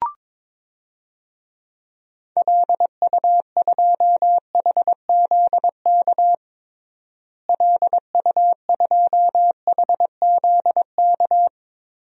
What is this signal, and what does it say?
Morse code, heard as a keyed tone, about 22 wpm, 700 Hz. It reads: LU2HZK LU2HZK
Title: DXCC-100 Call Signs - 6 Characters - Repeated 3x - 22wpm